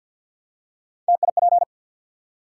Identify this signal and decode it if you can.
Morse code, heard as a keyed tone, about 49 wpm, 700 Hz. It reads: TIP
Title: Three Letter Words - 50wpm